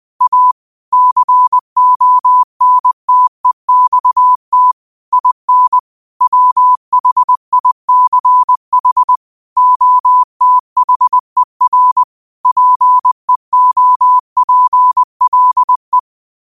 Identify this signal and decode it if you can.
Morse code, heard as a keyed tone, about 20 wpm, 1000 Hz. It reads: A CONTEXT IN WHICH OTHER PEOPLE
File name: qotdfiles / 20wpm